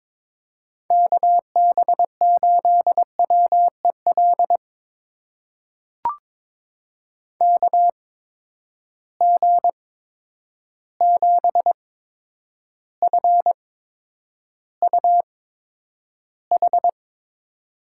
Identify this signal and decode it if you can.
Morse code, heard as a keyed tone, about 22 wpm, spaced as for 9 wpm, 700 Hz. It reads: KB8WEL KG7FUH
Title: USA Call Signs - 6 Characters - 8x Character Spacing - 22wpm